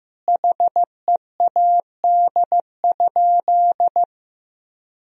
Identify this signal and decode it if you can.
Morse code, heard as a keyed tone, about 15 wpm, 700 Hz. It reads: HEAD?
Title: Sentences from Top 1000 Words - Review Part 1 - 15wpm